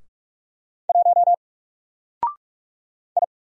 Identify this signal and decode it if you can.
Morse code, heard as a keyed tone, about 45 wpm, 700 Hz. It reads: 1 I
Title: Single Letter-Number 45wpm